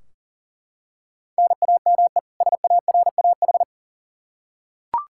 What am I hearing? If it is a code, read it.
Morse code, heard as a keyed tone, about 38 wpm, 700 Hz. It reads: NAME SARAH
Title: K1USN SST QSO Practice 40wpm